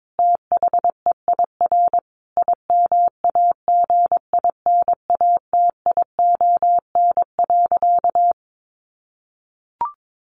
Morse code, heard as a keyed tone, about 22 wpm, 700 Hz. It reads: THEIR IMAGINATION.
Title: Sentences from Top 1300 Words 22wpm